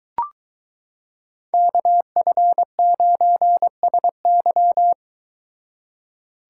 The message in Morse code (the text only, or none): KF9SY